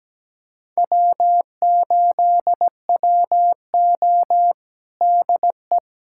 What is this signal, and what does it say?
Morse code, heard as a keyed tone, about 17 wpm, 700 Hz. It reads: W8WO DE
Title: USA Callsign DE Callsign - 4 Characters - 17wpm